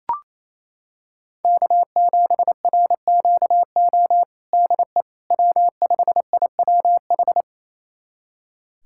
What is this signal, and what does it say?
Morse code, heard as a keyed tone, about 28 wpm, 700 Hz. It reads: K7RQO DE W5IWH
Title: USA Callsign DE Callsign - 5 Characters - 28wpm